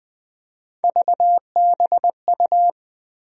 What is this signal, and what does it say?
Morse code, heard as a keyed tone, about 20 wpm, 700 Hz. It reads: VBU
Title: USA Call Sign Postfixes - 3 Letters 20wpm